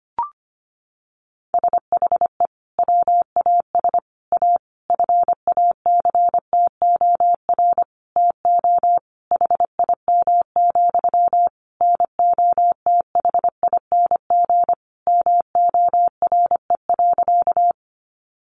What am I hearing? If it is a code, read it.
Morse code, heard as a keyed tone, about 25 wpm, 700 Hz. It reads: SHE WAS A FACTOR TO HIM, NOTHING MORE.
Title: Sentences from Top 1300 Words 25wpm